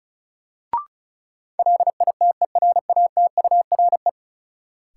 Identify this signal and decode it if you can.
Morse code, heard as a keyed tone, about 36 wpm, 700 Hz. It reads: LITERATURE
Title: Sentences from Top 600 Words - New Word Drill - 35wpm